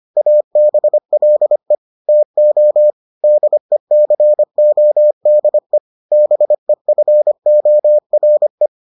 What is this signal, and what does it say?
Morse code, heard as a keyed tone, about 25 wpm, 600 Hz. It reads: ABLE TO DECODE BEFORE